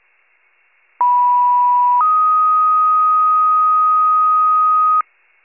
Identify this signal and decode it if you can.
Two-tone page, heard as a one-second tone, then a three-second tone, about 980 then 1250 Hz. It is heard from 1.0 to 5.0 s.
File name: pager / 91 FIRE / wcalert